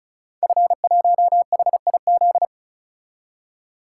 Morse code, heard as a keyed tone, about 36 wpm, 700 Hz. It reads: F1HIZ
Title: Code Groups - 5 Letter:Numbers - 35wpm